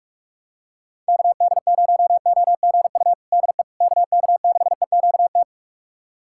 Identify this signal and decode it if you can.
Morse code, heard as a keyed tone, about 42 wpm, 700 Hz. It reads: KD0OGU DE KK6EQT
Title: USA Callsign DE Callsign - 6 Characters - 45wpm